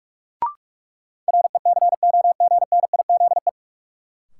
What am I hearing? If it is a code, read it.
Morse code, heard as a keyed tone, about 45 wpm, 700 Hz. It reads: RECOGNIZE
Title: Sentences from Top 1400 Words - New Word Drill - 45wpm